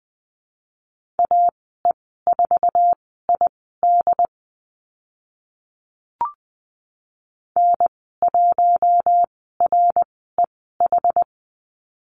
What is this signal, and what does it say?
Morse code, heard as a keyed tone, about 20 wpm, spaced as for 14 wpm, 700 Hz. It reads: AE4ID N1REH
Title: 082 Lesson 14g - Number 2 - Call Signs 20wpm